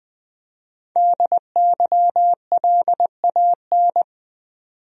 Morse code, heard as a keyed tone, about 20 wpm, 700 Hz. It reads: DYLAN
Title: USA Common Names - Max 5 Letters - 20wpm